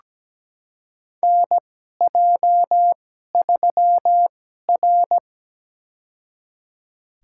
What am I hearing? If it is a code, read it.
Morse code, heard as a keyed tone, about 17 wpm, spaced as for 12 wpm, 700 Hz. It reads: NJ3R
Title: USA Call Signs - 4 Characters - 2x Character Spacing - 17wpm